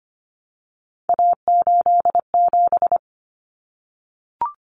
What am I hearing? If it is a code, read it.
Morse code, heard as a keyed tone, about 25 wpm, 700 Hz. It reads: A87
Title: Common Cut Numbers - 3-Digit 25wpm